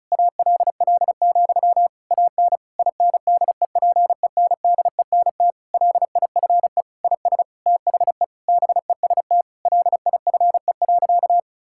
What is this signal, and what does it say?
Morse code, heard as a keyed tone, about 36 wpm, 700 Hz. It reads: ALL, AN INDEPENDENT LIFE IS THE BEST LIFE.
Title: Sentences from Top 1000 Words - Review Part 6 - 35wpm